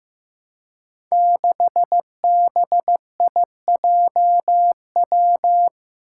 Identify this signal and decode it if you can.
Morse code, heard as a keyed tone, about 15 wpm, 700 Hz. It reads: 6BIJW
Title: Five Challenging Letter-Number Combinations 15wpm